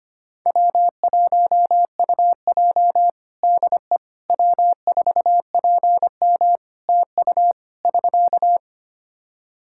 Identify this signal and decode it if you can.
Morse code, heard as a keyed tone, about 25 wpm, 700 Hz. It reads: W1UJ DE W4PM TU <SK>
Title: K1USN SST QSO Practice 25wpm